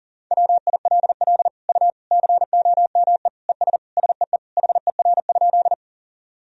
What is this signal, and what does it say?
Morse code, heard as a keyed tone, about 40 wpm, 700 Hz. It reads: WILL U COME ES SEE HER?
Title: Sentences from Top 100 Words and Abbreviations 40wpm